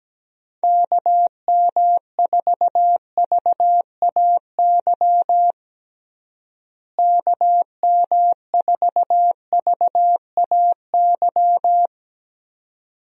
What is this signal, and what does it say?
Morse code, heard as a keyed tone, about 17 wpm, 700 Hz. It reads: KM4VAY KM4VAY
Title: USA Call Signs - 6 Characters - Repeated 2x - 17wpm